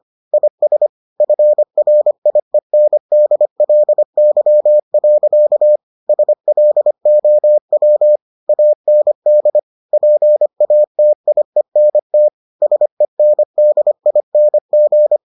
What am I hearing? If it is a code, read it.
Morse code, heard as a keyed tone, about 25 wpm, 600 Hz. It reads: IS FRIENDLY. SLOW AND PATIENT SENDING